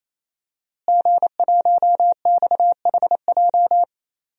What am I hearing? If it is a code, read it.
Morse code, heard as a keyed tone, about 28 wpm, 700 Hz. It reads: G1XHJ
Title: DXCC-100 Call Signs - 5 Characters - Repeated 3x - 28wpm